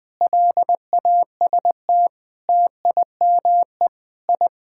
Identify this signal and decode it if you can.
Morse code, heard as a keyed tone, about 20 wpm, 700 Hz. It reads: LAST TIME I
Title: Sentences with 7-Word Phrases - 20wpm